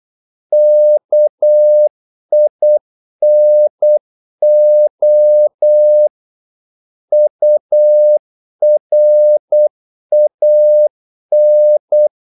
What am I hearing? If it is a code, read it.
Morse code, heard as a keyed tone, about 8 wpm, 600 Hz. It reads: KINO URAN
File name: Korte ord DK 8wpm